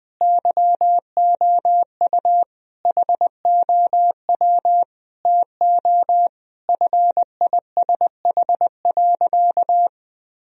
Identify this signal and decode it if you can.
Morse code, heard as a keyed tone, about 20 wpm, 700 Hz. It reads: YOU HOW TO FISH.
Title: Sentences from Top 1000 Words - Review Part 1 - 20wpm